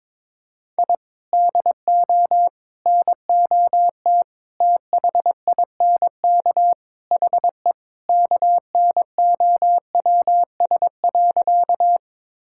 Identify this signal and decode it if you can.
Morse code, heard as a keyed tone, about 22 wpm, 700 Hz. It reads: I DO NOT THINK HE KNOWS.
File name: practice / 22wpm / Sentences with 4-Word Phrases - 22wpm